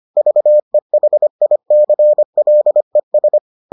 Morse code, heard as a keyed tone, about 25 wpm, 600 Hz. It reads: VEHICLES